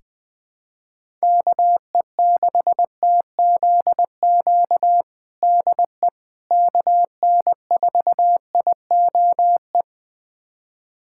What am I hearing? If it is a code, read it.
Morse code, heard as a keyed tone, about 20 wpm, 700 Hz. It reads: KE6TZQ DE KN4IOE
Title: USA Callsign DE Callsign 20wpm